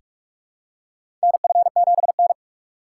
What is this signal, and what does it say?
Morse code, heard as a keyed tone, about 46 wpm, 700 Hz. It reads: NF7N
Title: USA Call Signs - 4 Characters - 4x Character Spacing - 45wpm